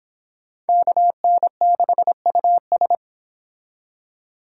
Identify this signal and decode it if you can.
Morse code, heard as a keyed tone, about 30 wpm, 700 Hz. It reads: KN6US